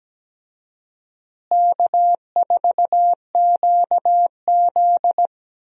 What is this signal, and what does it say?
Morse code, heard as a keyed tone, about 17 wpm, 700 Hz. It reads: K4QZ
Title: USA Call Signs - 4 Characters - Repeated 3x - 17wpm